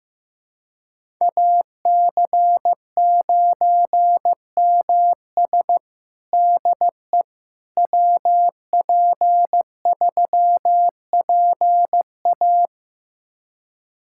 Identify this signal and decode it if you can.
Morse code, heard as a keyed tone, about 15 wpm, 700 Hz. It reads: AC9MS DE WP3PA
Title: USA Callsign DE Callsign - 5 Characters - 15wpm